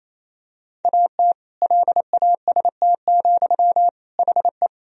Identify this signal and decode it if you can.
Morse code, heard as a keyed tone, about 28 wpm, 700 Hz. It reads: AT LAST, HE
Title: Sentences from Top 1000 Words - Review Part 1 - 28wpm